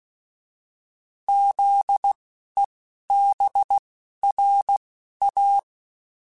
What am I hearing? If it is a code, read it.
Morse code, heard as a keyed tone, about 16 wpm, spaced as for 11 wpm, 800 Hz. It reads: ZEBRA